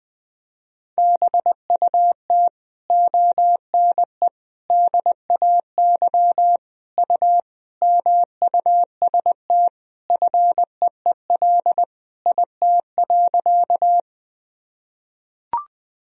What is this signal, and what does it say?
Morse code, heard as a keyed tone, about 20 wpm, 700 Hz. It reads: BUT ONE DAY U MUST FEEL IT.
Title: Sentences from Top 400 Words and Abbreviations 20wpm